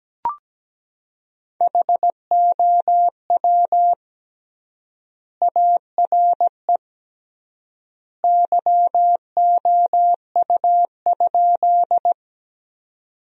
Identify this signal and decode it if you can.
Morse code, heard as a keyed tone, about 17 wpm, 700 Hz. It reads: HOW ARE YOU?
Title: 3x Word-Spacing - Sentences from Top 100 Words - 17wpm